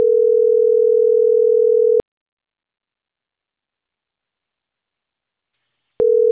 Ringback tone (North America), 440 + 480 Hz.